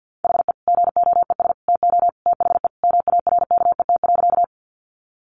Morse code, heard as a keyed tone, about 50 wpm, 700 Hz. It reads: HE GOES TO THE MARKET.
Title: Additional Easy Phrases 50wpm